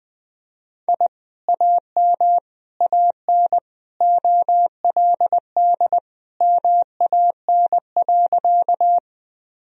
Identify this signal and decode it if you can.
Morse code, heard as a keyed tone, about 20 wpm, 700 Hz. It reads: I AM AN OLD MAN.
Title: Sentences from Top 1000 Words - Review Part 2 - 20wpm